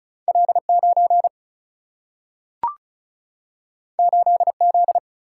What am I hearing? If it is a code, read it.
Morse code, heard as a keyed tone, about 36 wpm, 700 Hz. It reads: L 9 8 Z
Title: Single-Letter Single-Number 35wpm